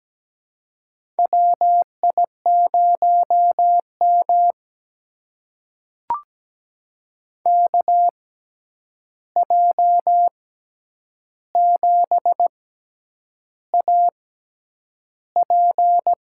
Morse code, heard as a keyed tone, about 17 wpm, spaced as for 8 wpm, 700 Hz. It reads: WI0M KJ7AP
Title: USA Call Signs - 6x Character Spacing - 17wpm